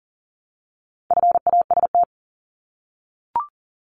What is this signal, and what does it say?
Morse code, heard as a keyed tone, about 40 wpm, 700 Hz. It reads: FAST
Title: Top 500 Words Repeated 40wpm